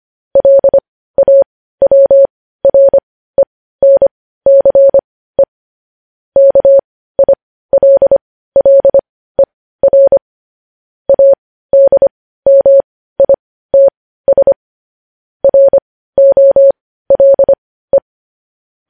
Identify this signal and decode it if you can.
Morse code, heard as a keyed tone, about 25 wpm, spaced as for 15 wpm, 550 Hz. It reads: LAWRENCE KILLER ADMITS ROLE
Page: news headlines in morse code at 15 wpm